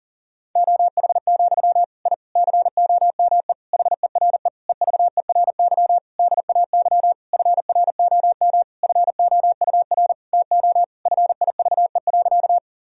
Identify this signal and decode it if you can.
Morse code, heard as a keyed tone, about 37 wpm, 700 Hz. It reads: OH, I COME HERE EVERY DAY FROM FOUR TO FIVE.
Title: Sentences from Top 1000 Words - Review Part 2 - 40wpm